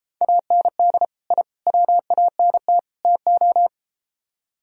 Morse code, heard as a keyed tone, about 33 wpm, 700 Hz. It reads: AND I WANT TO